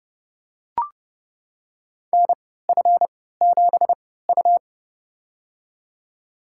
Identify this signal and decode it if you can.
Morse code, heard as a keyed tone, about 30 wpm, spaced as for 17 wpm, 700 Hz. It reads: NF7U
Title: USA Call Signs - 3x Character Spacing - 30wpm